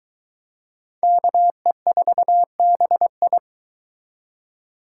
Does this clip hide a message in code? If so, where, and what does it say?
Yes: Morse code from 1.0 to 3.4 s, heard as a keyed tone, about 23 wpm, 700 Hz. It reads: KE4BI